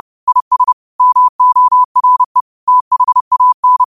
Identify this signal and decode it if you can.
Morse code, heard as a keyed tone, about 30 wpm, 1000 Hz. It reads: IS MORE THAN
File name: qotdfiles / 30wpm